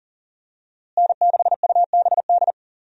Morse code, heard as a keyed tone, about 40 wpm, 700 Hz. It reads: N6UBD
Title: USA Call Signs - 5 Characters - Repeated 3x - 40wpm